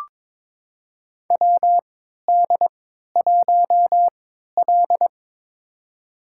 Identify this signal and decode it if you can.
Morse code, heard as a keyed tone, about 22 wpm, spaced as for 13 wpm, 700 Hz. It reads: WD1L
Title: USA Call Signs - 4 Characters - 3x Character Spacing - 22wpm